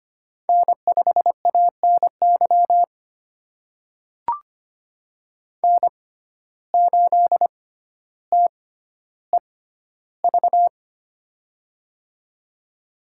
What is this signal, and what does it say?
Morse code, heard as a keyed tone, about 25 wpm, spaced as for 13 wpm, 700 Hz. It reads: N5ANY N8TEV
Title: USA Call Signs - 6x Character Spacing - 25wpm